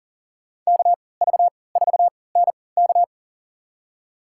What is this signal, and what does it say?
Morse code, heard as a keyed tone, about 40 wpm, spaced as for 23 wpm, 700 Hz. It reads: KV4NK
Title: USA Call Signs - 3x Character Spacing - 40wpm